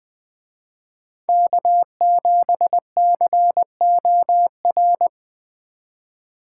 Morse code, heard as a keyed tone, about 20 wpm, 700 Hz. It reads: K7COR